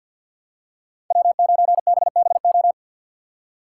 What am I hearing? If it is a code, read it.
Morse code, heard as a keyed tone, about 51 wpm, 700 Hz. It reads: W9BBO
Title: USA Call Signs - 5 Characters - 2x Character Spacing - 50wpm